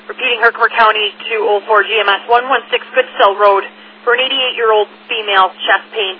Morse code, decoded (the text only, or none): none